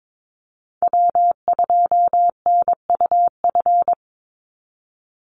Morse code, heard as a keyed tone, about 22 wpm, 700 Hz. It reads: W2NUF
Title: USA Call Signs 22wpm